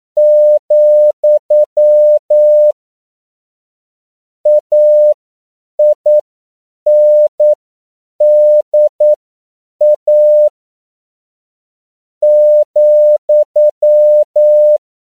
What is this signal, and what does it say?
Morse code, heard as a keyed tone, about 9 wpm, spaced as for 7 wpm, 600 Hz. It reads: , AINDA ,
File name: CW / 10PPM